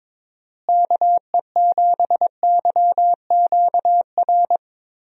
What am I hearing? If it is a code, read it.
Morse code, heard as a keyed tone, about 22 wpm, 700 Hz. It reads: KE7YQR
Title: USA Call Signs - 6 Characters - Repeated 2x - 22wpm